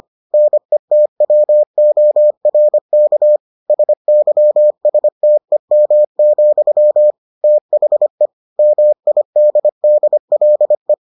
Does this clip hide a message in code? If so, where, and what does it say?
Yes: Morse code from 0.3 to 10.9 s, heard as a keyed tone, about 25 wpm, 600 Hz. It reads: NETWORK SYSTEM, THE MIDDLE